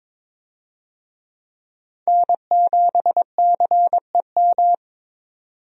Morse code, heard as a keyed tone, about 21 wpm, 700 Hz. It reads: N7CEM
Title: USA Call Signs - 5 Characters - 22wpm